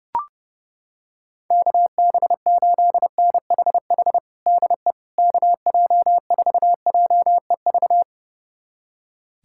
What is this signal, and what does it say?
Morse code, heard as a keyed tone, about 30 wpm, 700 Hz. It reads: KB8NHH DE KJ4JEV
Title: USA Callsign DE Callsign 30wpm